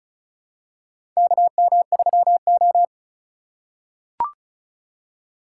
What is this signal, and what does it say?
Morse code, heard as a keyed tone, about 36 wpm, 700 Hz. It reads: KM3O
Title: USA Call Signs - 4 Characters - 4x Character Spacing - 35wpm